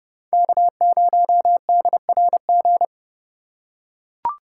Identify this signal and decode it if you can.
Morse code, heard as a keyed tone, about 29 wpm, 700 Hz. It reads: K0DRG